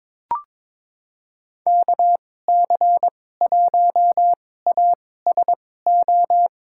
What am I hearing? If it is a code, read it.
Morse code, heard as a keyed tone, about 22 wpm, spaced as for 16 wpm, 700 Hz. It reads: KC1ASO
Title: USA Call Signs - 6 Characters - 2x Character Spacing - 22wpm